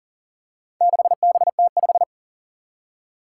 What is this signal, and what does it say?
Morse code, heard as a keyed tone, about 40 wpm, 700 Hz. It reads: 6BT5
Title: Four Challenging Letter-Number Combinations 40wpm